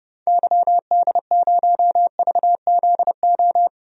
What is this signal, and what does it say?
Morse code, heard as a keyed tone, about 30 wpm, 700 Hz. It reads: YD0VZO
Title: DXCC-100 Call Signs - 6 Characters - 30wpm